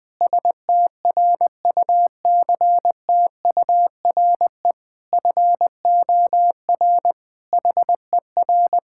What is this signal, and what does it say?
Morse code, heard as a keyed tone, about 20 wpm, 700 Hz. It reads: STRUCTURE FOR HER
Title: Sentences from Top 1000 Words - Review Part 6 - 20wpm